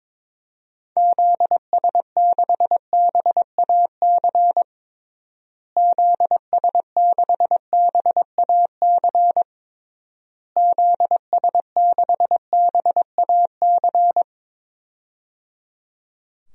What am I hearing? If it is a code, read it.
Morse code, heard as a keyed tone, about 22 wpm, 700 Hz. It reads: ZS6BAC ZS6BAC ZS6BAC
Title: DXCC-100 Call Signs - 6 Characters - Repeated 3x - 22wpm